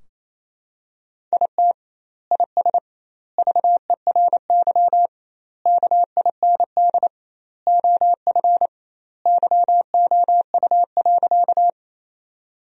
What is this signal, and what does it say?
Morse code, heard as a keyed tone, about 28 wpm, 700 Hz. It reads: IT IS VERY KIND OF YOU.
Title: 2x Word-Spacing - Sentences from Top 200 Words - 28wpm